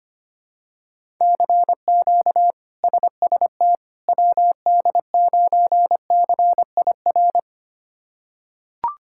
Morse code, heard as a keyed tone, about 25 wpm, 700 Hz. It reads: CQ SST WD9CIR
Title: K1USN SST QSO Practice - 25wpm